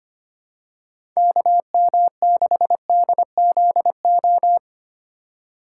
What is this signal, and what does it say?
Morse code, heard as a keyed tone, about 25 wpm, 700 Hz. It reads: KM6DZO
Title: USA Call Signs - 2x Character Spacing - 25wpm